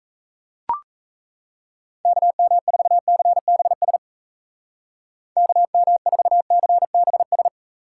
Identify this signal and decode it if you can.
Morse code, heard as a keyed tone, about 35 wpm, 700 Hz. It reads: KM4CBS KM4CBS
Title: Speed Racing - USA 2x3 Call Signs 28wpm